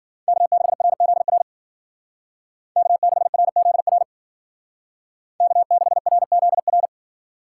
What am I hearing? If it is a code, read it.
Morse code, heard as a keyed tone, about 48 wpm, 700 Hz. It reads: K6RZR K6RZR K6RZR
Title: Speed Racing - USA 1x3 Call Signs 40wpm